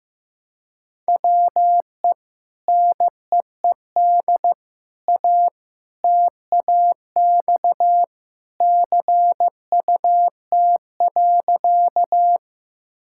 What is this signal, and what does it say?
Morse code, heard as a keyed tone, about 16 wpm, 700 Hz. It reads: WE NEED A TAX CUT.
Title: Sentences from Top 1000 Words - Review Part 5 - 15wpm